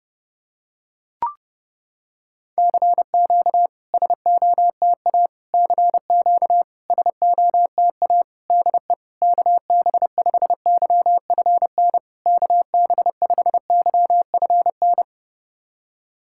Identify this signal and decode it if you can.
Morse code, heard as a keyed tone, about 30 wpm, 700 Hz. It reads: CQ SOTA CQ SOTA DE KB5YFN KB5YFN
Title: SOTA QSO Practice 30wpm